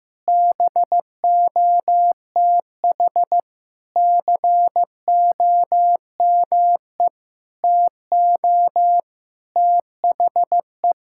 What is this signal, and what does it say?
Morse code, heard as a keyed tone, about 15 wpm, 700 Hz. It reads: BOTH COME TO THE
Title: Sentences from Top 1600 Words 15wpm